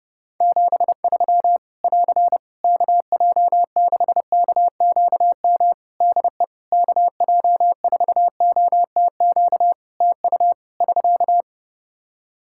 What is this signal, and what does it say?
Morse code, heard as a keyed tone, about 30 wpm, 700 Hz. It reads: 73 <AR> KJ6KQM DE KJ4OTQ TU <SK>
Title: Practice QSO - Exchange 4 - Ending - 30wpm